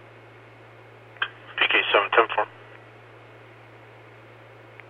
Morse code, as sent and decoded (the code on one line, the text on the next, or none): none